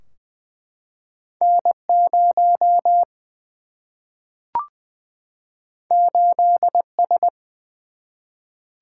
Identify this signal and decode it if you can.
Morse code, heard as a keyed tone, about 20 wpm, 700 Hz. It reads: N0 8S
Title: Single-Letter Single-Number 20wpm